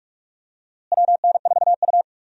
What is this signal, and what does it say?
Morse code, heard as a keyed tone, about 48 wpm, 700 Hz. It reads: WN4U
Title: USA Call Signs - 4 Characters - Repeated 2x - 45wpm